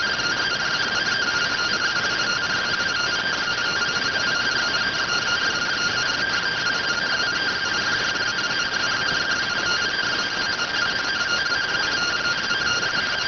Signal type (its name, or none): two-tone teleprinter signal